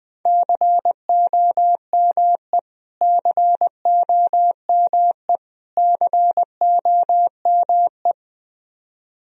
Morse code, heard as a keyed tone, about 20 wpm, 700 Hz. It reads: COME COME COME
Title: Top 100 Words Repeated 20wpm